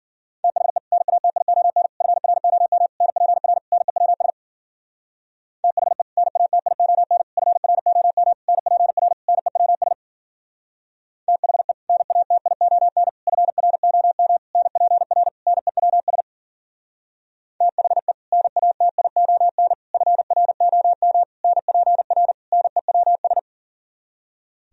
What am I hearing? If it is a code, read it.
Morse code, heard as a keyed tone, about 48 wpm, 700 Hz. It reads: THE NATION FROM NPR NEWS THE NATION FROM NPR NEWS THE NATION FROM NPR NEWS THE NATION FROM NPR NEWS
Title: Speed-Racing - Sets of 5 Words 40wpm